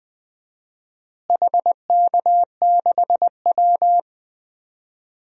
Morse code, heard as a keyed tone, about 20 wpm, 700 Hz. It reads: HK6W
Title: DXCC-100 Call Signs - 4 Characters - Repeated 2x - 20wpm